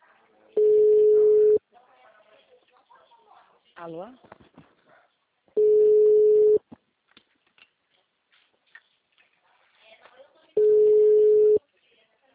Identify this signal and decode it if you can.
Ringback tone (Europe, 425 Hz), with speech.